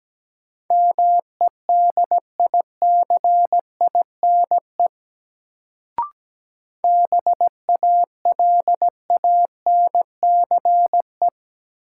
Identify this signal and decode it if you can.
Morse code, heard as a keyed tone, about 17 wpm, 700 Hz. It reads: MEDICINE BALANCE
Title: Sentences from Top 900 Words - New Word Drill - 17wpm